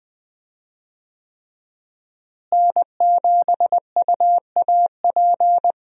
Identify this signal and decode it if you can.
Morse code, heard as a keyed tone, about 20 wpm, 700 Hz. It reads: N7UAP
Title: USA Call Signs - 5 Characters - Repeated 3x - 20wpm